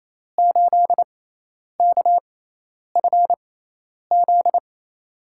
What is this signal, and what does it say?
Morse code, heard as a keyed tone, about 28 wpm, spaced as for 10 wpm, 700 Hz. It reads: 8KFZ
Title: USA Call Signs - 6 Characters - 6x Character Spacing - 28wpm